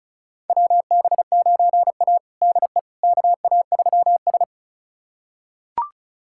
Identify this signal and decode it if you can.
Morse code, heard as a keyed tone, about 36 wpm, 700 Hz. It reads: WB9A DE KA3S
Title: USA Callsign DE Callsign - 4 Characters - 35wpm